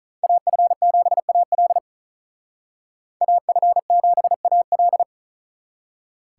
Morse code, heard as a keyed tone, about 39 wpm, 700 Hz. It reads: AF7AL AF7AL
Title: Speed Racing - USA 2x2 Call Signs 35wpm